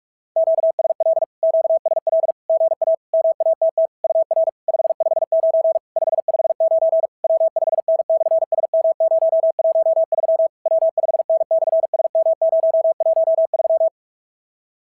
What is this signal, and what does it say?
Morse code, heard as a keyed tone, about 44 wpm, 660 Hz. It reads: QSL QSL GA MATT UR 559 559 W5N/SM013 W5N/SM013
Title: SOTA QSO Practice 45wpm